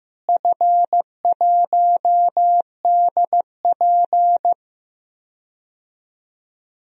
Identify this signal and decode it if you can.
Morse code, heard as a keyed tone, about 15 wpm, 700 Hz. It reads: F1DP
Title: DXCC-100 Call Signs - 4 Characters - Repeated 2x - 15wpm